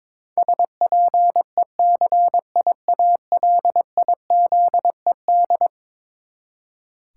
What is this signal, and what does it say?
Morse code, heard as a keyed tone, about 22 wpm, 700 Hz. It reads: SPECIALIZED
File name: practice / 22wpm / Eleven Letter Words - 22wpm